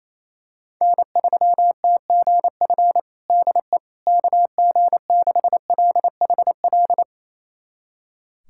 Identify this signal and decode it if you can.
Morse code, heard as a keyed tone, about 28 wpm, 700 Hz. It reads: N3TGF DE KG6LHL
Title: USA Callsign DE Callsign 28wpm